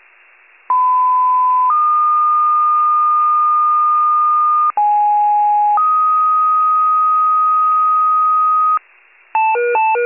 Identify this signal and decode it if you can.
Two-tone page, heard as a one-second tone, then a three-second tone, about 1010 then 1220 Hz, about 830 then 1250 Hz. It is heard from 0.7 to 8.8 s.